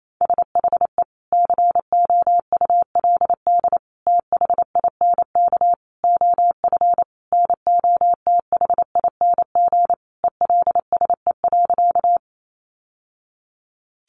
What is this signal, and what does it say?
Morse code, heard as a keyed tone, about 28 wpm, 700 Hz. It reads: SHE COULD THINK OF NOTHING ELSE.
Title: Sentences from Top 800 Words 28wpm